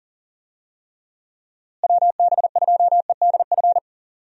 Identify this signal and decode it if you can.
Morse code, heard as a keyed tone, about 40 wpm, 700 Hz. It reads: WB2EDF
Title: USA Call Signs - 6 Characters - Repeated 2x - 40wpm